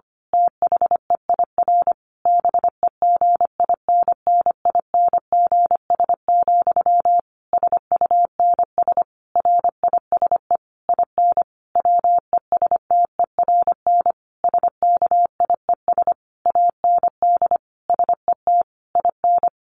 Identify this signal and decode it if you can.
Morse code, heard as a keyed tone, about 25 wpm, 700 Hz. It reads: THEIR BEGINNINGS, SUNS RISE IN WESTERN SKIES AND SET IN
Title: A Sound Of Thunder 25wpm